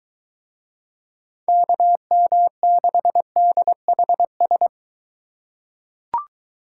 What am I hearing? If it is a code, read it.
Morse code, heard as a keyed tone, about 23 wpm, 700 Hz. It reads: KM6DHS